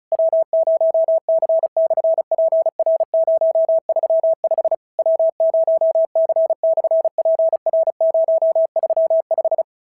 Morse code, heard as a keyed tone, about 36 wpm, 650 Hz. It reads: W0C/PR035 W0C/PR035
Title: SOTA QSO Practice 35wpm